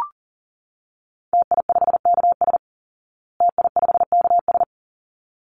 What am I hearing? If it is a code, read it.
Morse code, heard as a keyed tone, about 40 wpm, 700 Hz. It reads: TI5KS TI5KS
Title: Call Signs Repeated 3x - 40wpm